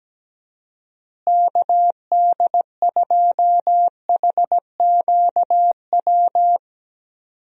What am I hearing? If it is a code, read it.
Morse code, heard as a keyed tone, about 17 wpm, 700 Hz. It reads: KD2HQW
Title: USA Call Signs - 6 Characters - Repeated 2x - 17wpm